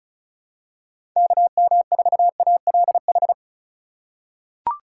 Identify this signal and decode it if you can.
Morse code, heard as a keyed tone, about 36 wpm, 700 Hz. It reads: KM4ALH
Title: USA Call Signs - 6 Characters - 2x Character Spacing - 35wpm